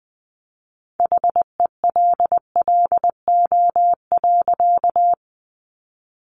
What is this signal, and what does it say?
Morse code, heard as a keyed tone, about 20 wpm, 700 Hz. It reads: HELLO.
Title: Sentences from Top 1100 Words 20wpm